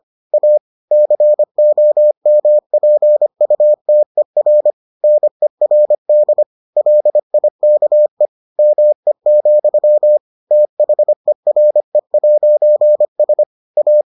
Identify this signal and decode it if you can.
Morse code, heard as a keyed tone, about 25 wpm, 600 Hz. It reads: A COMPUTER NERD LIKE ME, THERE'S A